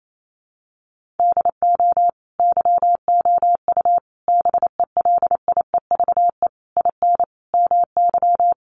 Morse code, heard as a keyed tone, about 28 wpm, 700 Hz. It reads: DO YOU BELIEVE IN MY
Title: Sentences from Top 1200 Words 28wpm